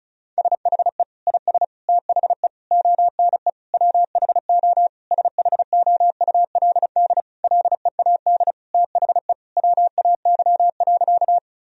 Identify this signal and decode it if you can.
Morse code, heard as a keyed tone, about 34 wpm, 700 Hz. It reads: SHE IS THE ONE WHO SHOULD LEAD THE WAY.
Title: Sentences from Top 1000 Words - Review Part 2 - 35wpm